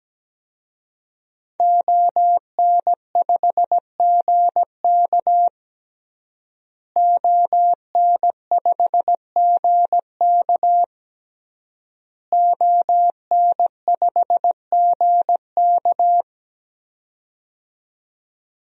Morse code, heard as a keyed tone, about 17 wpm, 700 Hz. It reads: ON5GK ON5GK ON5GK
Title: DXCC-100 Call Signs - 5 Characters - Repeated 3x - 17wpm